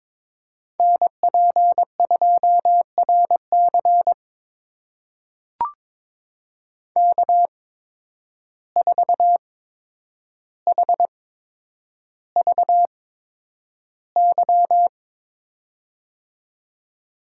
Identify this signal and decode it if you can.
Morse code, heard as a keyed tone, about 22 wpm, spaced as for 9 wpm, 700 Hz. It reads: NP2RC K4HVY
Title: USA Call Signs - 5 Characters - 8x Character Spacing - 22wpm